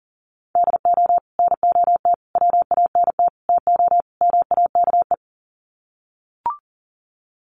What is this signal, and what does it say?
Morse code, heard as a keyed tone, about 40 wpm, 700 Hz. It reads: DO NOT WANT TO MAKE
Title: Sets of 5 Words Encore 40wpm